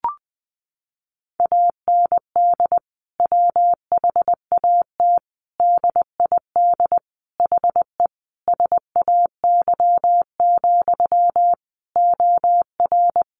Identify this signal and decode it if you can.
Morse code, heard as a keyed tone, about 20 wpm, 700 Hz. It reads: AND WHAT DID HE SAY, OR
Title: Sentences from Top 700 Words 20wpm